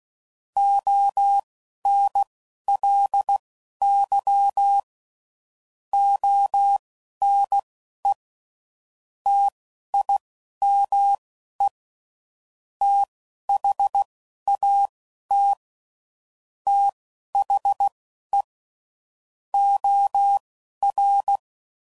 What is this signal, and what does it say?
Morse code, heard as a keyed tone, about 16 wpm, spaced as for 11 wpm, 800 Hz. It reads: ONLY ONE TIME THAT THE OR
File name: CW Code Groups @10WPM/Technical Subjects